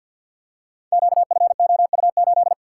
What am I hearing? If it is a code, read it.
Morse code, heard as a keyed tone, about 52 wpm, 700 Hz. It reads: QFOU8
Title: Code Groups - 5 Letter:Numbers - 50wpm